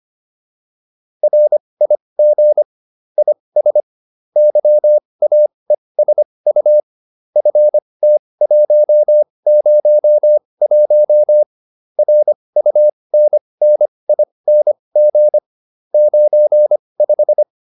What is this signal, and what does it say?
Morse code, heard as a keyed tone, about 25 wpm, spaced as for 20 wpm, 600 Hz. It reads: RIG IS YAESU FT101 RUNNING 95